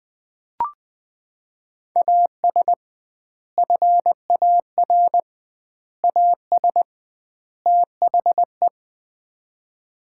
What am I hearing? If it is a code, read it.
Morse code, heard as a keyed tone, about 20 wpm, 700 Hz. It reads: AS FAR AS THE
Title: 2x Word-Spacing - Sets of 4 Words - 20wpm